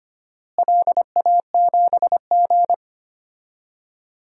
Morse code, heard as a keyed tone, about 25 wpm, 700 Hz. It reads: LA7G
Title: DXCC-100 Call Signs - 4 Characters - Repeated 3x - 25wpm